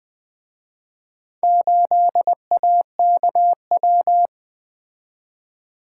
Morse code, heard as a keyed tone, about 24 wpm, 700 Hz. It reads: 8AKW
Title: Speed-Racing - Four Letter-Number Combinations - 20wpm